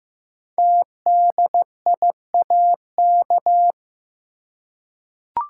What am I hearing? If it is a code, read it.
Morse code, heard as a keyed tone, about 15 wpm, 700 Hz. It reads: TDIAK
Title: Code Groups - 5 Letters - 15wpm